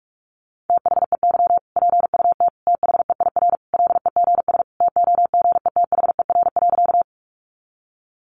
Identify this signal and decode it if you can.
Morse code, heard as a keyed tone, about 44 wpm, 700 Hz. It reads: THEY PUT THEIR LEGS TOGETHER.
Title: Easy Phrases 45wpm